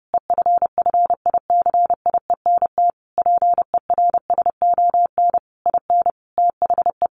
Morse code, heard as a keyed tone, about 28 wpm, 700 Hz. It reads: EFFICIENT PERSON IN THE
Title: Sentences from Top 1300 Words 30wpm